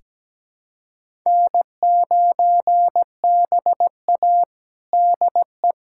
Morse code, heard as a keyed tone, about 17 wpm, 700 Hz. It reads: N9BA DE
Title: USA Callsign DE Callsign - 4 Characters - 17wpm